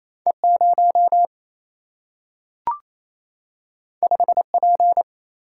Morse code, heard as a keyed tone, about 28 wpm, 700 Hz. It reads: E 0 5 P
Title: Single-Letter Single-Number 28wpm